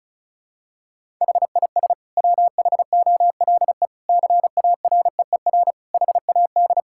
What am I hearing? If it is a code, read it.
Morse code, heard as a keyed tone, about 36 wpm, 700 Hz. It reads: HIS WHOLE CAREER HAD
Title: Sentences from Top 1000 Words - Review Part 6 - 35wpm